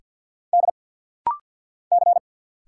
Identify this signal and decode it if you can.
Morse code, heard as a keyed tone, about 50 wpm, 700 Hz. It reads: D C
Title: Single Letters - Rapid-Fire 50wpm